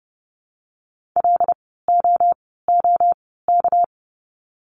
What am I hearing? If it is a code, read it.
Morse code, heard as a keyed tone, about 30 wpm, spaced as for 17 wpm, 700 Hz. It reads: LOOK
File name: course / 30:10wpm / 230 Lesson 37f - Letter x - Words 30wpm